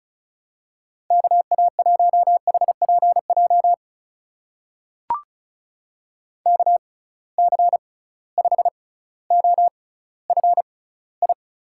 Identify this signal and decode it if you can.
Morse code, heard as a keyed tone, about 36 wpm, spaced as for 18 wpm, 700 Hz. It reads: KA1HPJ KC5OFI
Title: USA Call Signs - 6 Characters - 6x Character Spacing - 35wpm